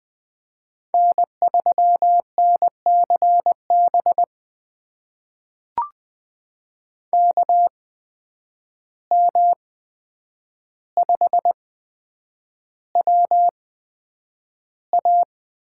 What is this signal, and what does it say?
Morse code, heard as a keyed tone, about 20 wpm, spaced as for 9 wpm, 700 Hz. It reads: N3NCB KM5WA
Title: USA Call Signs - 5 Characters - 8x Character Spacing - 20wpm